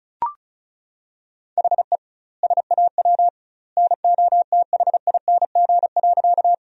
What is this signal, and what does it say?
Morse code, heard as a keyed tone, about 34 wpm, 700 Hz. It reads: HE SAW NOTHING.
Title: 2x Word-Spacing - Sentences from Top 500 Words - 35wpm